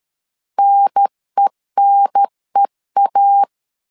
Morse code, heard as a keyed tone, about 13 wpm, 800 Hz. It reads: NENEA
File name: LEZIONI / 2 UHDN / 2 UHDN 12wpm 1456